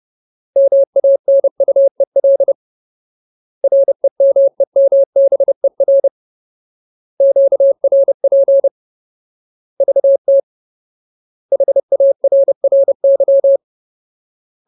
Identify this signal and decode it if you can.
Morse code, heard as a keyed tone, about 30 wpm, 550 Hz. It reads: MANUEL REMEMBER QRP VT HARRY
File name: Ultimate Word List / 30WPM Ultimate